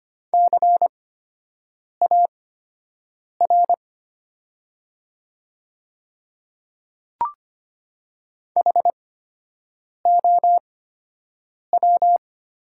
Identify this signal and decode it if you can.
Morse code, heard as a keyed tone, about 25 wpm, spaced as for 7 wpm, 700 Hz. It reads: CAR HOW
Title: 8x Character Spacing - Top 300 Words - 25wpm